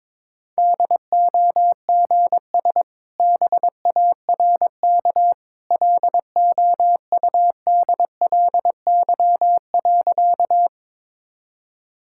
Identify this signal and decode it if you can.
Morse code, heard as a keyed tone, about 22 wpm, 700 Hz. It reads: DOGS BARK LOUDLY.
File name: practice / 22wpm / Easy Sentences with Hints - 22wpm